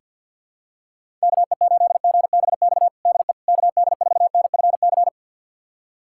Morse code, heard as a keyed tone, about 48 wpm, 700 Hz. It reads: KE8GBX DE KD4NFC
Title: USA Callsign DE Callsign - 6 Characters - 50wpm